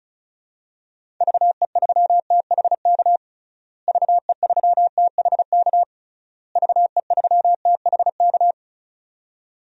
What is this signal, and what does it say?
Morse code, heard as a keyed tone, about 36 wpm, 700 Hz. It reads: VE3THK VE3THK VE3THK
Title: DXCC-100 Call Signs - 6 Characters - Repeated 3x - 35wpm